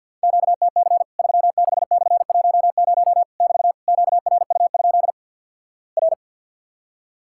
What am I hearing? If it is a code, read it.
Morse code, heard as a keyed tone, about 50 wpm, 700 Hz. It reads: QTC 36/10 <BT> QRU? R
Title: Worked All Europe DX Contest - QTC Exchange - 50wpm